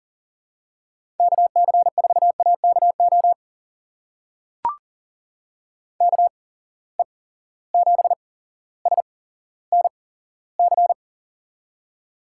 Morse code, heard as a keyed tone, about 40 wpm, spaced as for 17 wpm, 700 Hz. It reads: KC4AKO KE7SNC
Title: USA Call Signs - 6 Characters - 8x Character Spacing - 40wpm